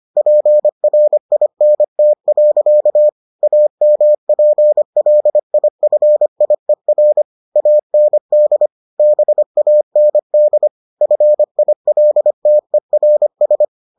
Morse code, heard as a keyed tone, about 25 wpm, 600 Hz. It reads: PRINT. AMPLIFIER AND BAND FILTERS